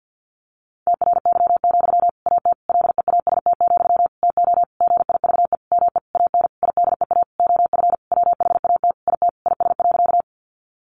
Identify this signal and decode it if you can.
Morse code, heard as a keyed tone, about 50 wpm, 700 Hz. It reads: TRY, AT LEAST, TO GIVE ME AN IDEA OF WHAT IT IS.
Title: Sentences from Top 700 Words 50wpm